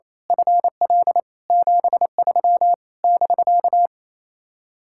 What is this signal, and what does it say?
Morse code, heard as a keyed tone, about 27 wpm, 700 Hz. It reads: FL 73 <BK>
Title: Speed-Racing Common QSO Phrases - 28wpm